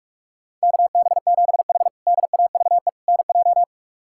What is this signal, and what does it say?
Morse code, heard as a keyed tone, about 43 wpm, 700 Hz. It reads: KB7H DAVE NJ
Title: K1USN SST QSO Practice - 45wpm